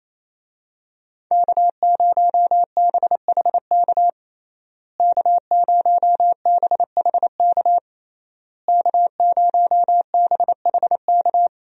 Morse code, heard as a keyed tone, about 28 wpm, 700 Hz. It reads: K0BHK K0BHK K0BHK
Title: USA Call Signs - 5 Characters - Repeated 3x - 28wpm